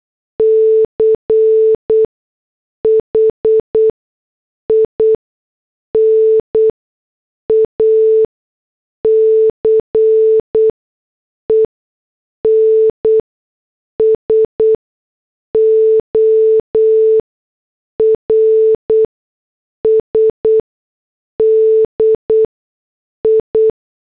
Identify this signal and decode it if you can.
Morse code, heard as a keyed tone, about 8 wpm, spaced as for 6 wpm, 440 Hz. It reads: CHINA CENSORS DI